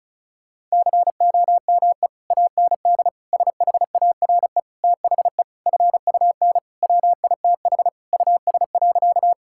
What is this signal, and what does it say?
Morse code, heard as a keyed tone, about 36 wpm, 700 Hz. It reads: COME AND SHARE THE FUN WITH US.
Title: Sentences from Top 1000 Words - Review Part 6 - 35wpm